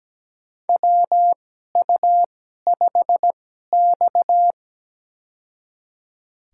Morse code, heard as a keyed tone, about 17 wpm, spaced as for 12 wpm, 700 Hz. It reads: WU5X
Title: USA Call Signs - 2x Character Spacing - 17wpm